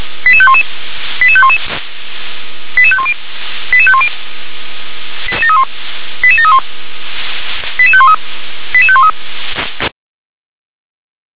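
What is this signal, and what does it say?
Five-tone selective call (ZVEI), heard as quick runs of five tones, tones of about 70 ms. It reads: 88511 88511 88510 88510 88731 88513 88513
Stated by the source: ZVEI five-tone call received by SDR.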